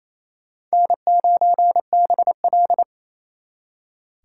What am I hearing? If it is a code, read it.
Morse code, heard as a keyed tone, about 35 wpm, 700 Hz. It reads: N9BL
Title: Speed Racing - USA 1x2 Call Signs 28wpm